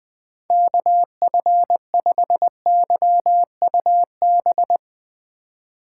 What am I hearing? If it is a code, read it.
Morse code, heard as a keyed tone, about 20 wpm, 700 Hz. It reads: KF5YUB
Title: K1USN SST QSO Practice - 20wpm